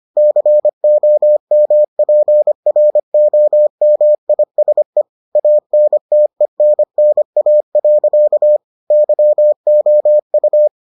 Morse code, heard as a keyed tone, about 25 wpm, 600 Hz. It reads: COMPROMISE ANTENNA. YOU